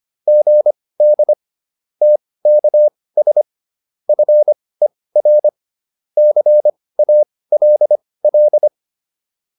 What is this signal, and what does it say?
Morse code, heard as a keyed tone, about 25 wpm, spaced as for 18 wpm, 600 Hz. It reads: GD TKS FER CALL